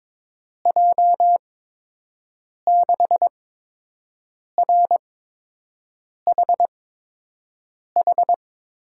Morse code, heard as a keyed tone, about 22 wpm, spaced as for 6 wpm, 700 Hz. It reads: J6RHH
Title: USA Call Signs - 6 Characters - 8x Character Spacing - 22wpm